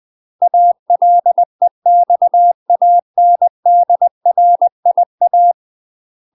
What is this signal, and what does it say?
Morse code, heard as a keyed tone, about 20 wpm, 700 Hz. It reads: ALEXANDRIA
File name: UK MASTER QSOs TRAINING / 20WPM / UK MASTER QSOs 0071